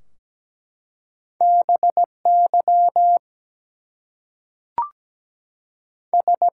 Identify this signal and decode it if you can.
Morse code, heard as a keyed tone, about 17 wpm, 700 Hz. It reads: BY S
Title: Two Letter Words - 17wpm